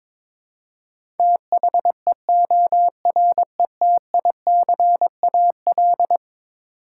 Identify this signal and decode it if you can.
Morse code, heard as a keyed tone, about 22 wpm, 700 Hz. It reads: THEORETICAL
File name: practice / 22wpm / Eleven Letter Words - 22wpm